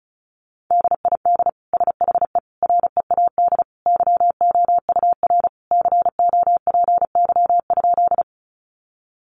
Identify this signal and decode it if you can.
Morse code, heard as a keyed tone, about 34 wpm, 700 Hz. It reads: DID SHE READ YOUR COPY?
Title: Sentences from Top 1400 Words 35wpm